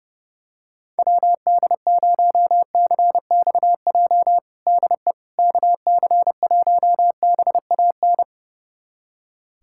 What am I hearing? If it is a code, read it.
Morse code, heard as a keyed tone, about 32 wpm, 700 Hz. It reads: WD0CXJ DE KC1BAN
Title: USA Callsign DE Callsign - 6 Characters - 30wpm